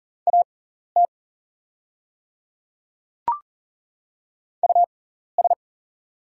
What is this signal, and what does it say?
Morse code, heard as a keyed tone, about 40 wpm, spaced as for 14 wpm, 690 Hz. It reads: AT US
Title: 6x Character Spacing - Top 300 Words - 40wpm